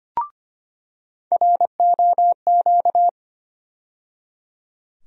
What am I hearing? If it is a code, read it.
Morse code, heard as a keyed tone, about 25 wpm, 700 Hz. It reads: ROQ
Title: USA Call Sign Postfixes - 3 Letters 25wpm